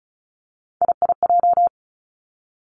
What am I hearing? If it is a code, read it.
Morse code, heard as a keyed tone, about 36 wpm, 700 Hz. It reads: IIJ
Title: Three Challenging Letter-Number Combinations 35wpm